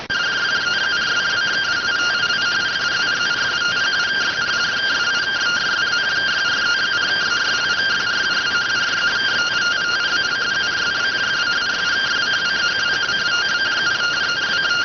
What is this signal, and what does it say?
Two-tone teleprinter signal, about 170 Hz shift.